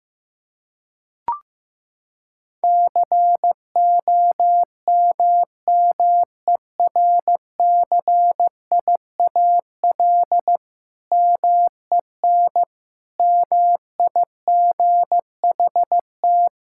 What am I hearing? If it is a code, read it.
Morse code, heard as a keyed tone, about 15 wpm, 700 Hz. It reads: COMMERCIAL MEN MIGHT
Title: Sentences from Top 1000 Words - Review Part 5 - 15wpm